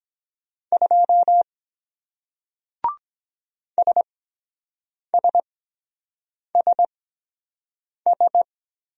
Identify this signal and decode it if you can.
Morse code, heard as a keyed tone, about 23 wpm, spaced as for 7 wpm, 700 Hz. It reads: 2 SSSS